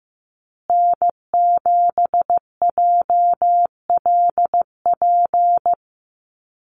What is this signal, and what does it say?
Morse code, heard as a keyed tone, about 15 wpm, 700 Hz. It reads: N7JLP
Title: USA Call Signs Repeated 2x 15wpm